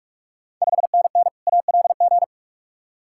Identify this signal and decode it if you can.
Morse code, heard as a keyed tone, about 46 wpm, 700 Hz. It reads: 5NN ALG
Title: Ontario QSO Party Exchange 45wpm